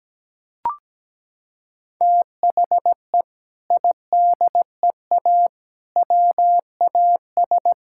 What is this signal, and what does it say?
Morse code, heard as a keyed tone, about 17 wpm, 700 Hz. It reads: THE IDEA WAS
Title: Sentences from Top 1000 Words - Review Part 6 - 17wpm